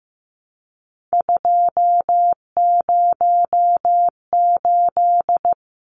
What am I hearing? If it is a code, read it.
Morse code, heard as a keyed tone, about 15 wpm, 700 Hz. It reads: 208
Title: Numbers - 3-Digit 15wpm